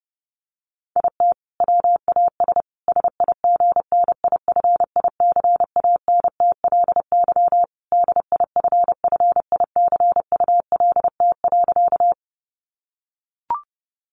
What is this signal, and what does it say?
Morse code, heard as a keyed tone, about 30 wpm, 700 Hz. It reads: IT WAS SIGNIFICANTLY DIFFICULT.
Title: Sentences from Top 1500 Words 30wpm